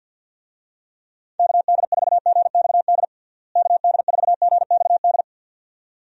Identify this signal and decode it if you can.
Morse code, heard as a keyed tone, about 50 wpm, 700 Hz. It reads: KD4GXD KD4GXD
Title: USA Call Signs - 6 Characters - Repeated 2x - 50wpm